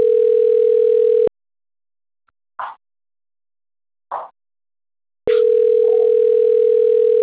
Ringback tone (North America), 440 + 480 Hz.